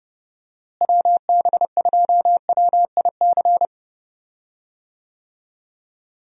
Morse code, heard as a keyed tone, about 30 wpm, 700 Hz. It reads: WB2WIC
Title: USA Call Signs - 6 Characters - Repeated 3x - 30wpm